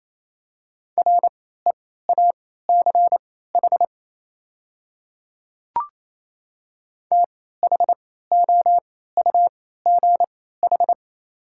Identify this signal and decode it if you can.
Morse code, heard as a keyed tone, about 28 wpm, spaced as for 16 wpm, 690 Hz. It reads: REACH THOUGH
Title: 3x Character Spacing - Top 500 Words - 28wpm